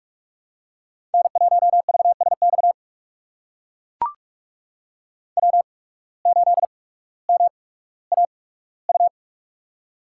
Morse code, heard as a keyed tone, about 44 wpm, spaced as for 19 wpm, 690 Hz. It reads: N1VSX W8MAU
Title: USA Call Signs - 5 Characters - 8x Character Spacing - 45wpm